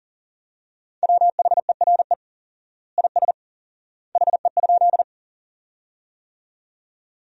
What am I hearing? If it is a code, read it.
Morse code, heard as a keyed tone, about 40 wpm, 700 Hz. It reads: WHERE IS HE?
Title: 4x Word-Spacing - Sentences from Top 200 Words - 40wpm